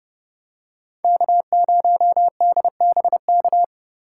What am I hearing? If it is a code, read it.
Morse code, heard as a keyed tone, about 29 wpm, 700 Hz. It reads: K0DBK